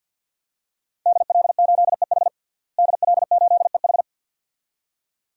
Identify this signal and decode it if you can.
Morse code, heard as a keyed tone, about 47 wpm, 700 Hz. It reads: DL8EH DL8EH
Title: DXCC-100 Call Signs - 5 Characters - Repeated 2x - 50wpm